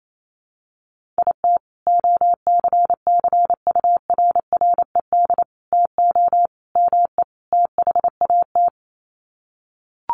Morse code, heard as a keyed tone, about 28 wpm, 700 Hz. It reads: IT OCCURRED TO ME THAT
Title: Sets of 5 Words Encore 28wpm